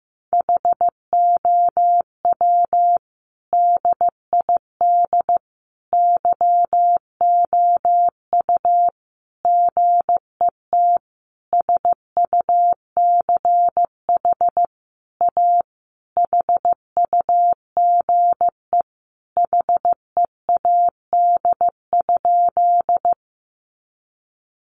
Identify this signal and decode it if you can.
Morse code, heard as a keyed tone, about 15 wpm, 700 Hz. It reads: HOW DID YOU GET SUCH A HUGE HEAD?
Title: Sentences from Top 1100 Words 15wpm